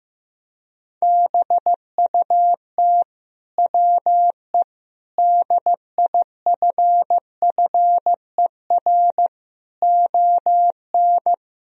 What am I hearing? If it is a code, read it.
Morse code, heard as a keyed tone, about 15 wpm, 700 Hz. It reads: BUT WE DIFFER ON
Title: Sentences from Top 1000 Words - Review Part 1 - 15wpm